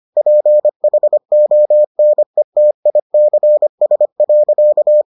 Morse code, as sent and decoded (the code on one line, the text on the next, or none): .--. .... --- -. - .. -.-. ... .-.-.-
PHONETICS.